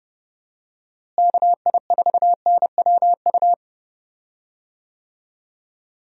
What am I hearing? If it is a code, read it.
Morse code, heard as a keyed tone, about 29 wpm, 700 Hz. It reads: KI4NWU
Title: USA Call Signs - 6 Characters - Repeated 3x - 30wpm